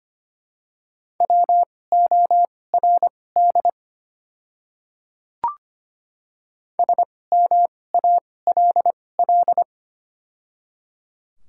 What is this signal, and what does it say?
Morse code, heard as a keyed tone, about 25 wpm, spaced as for 18 wpm, 700 Hz. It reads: WORD SMALL
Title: 2x Character Spacing - Top 300 Words - 25wpm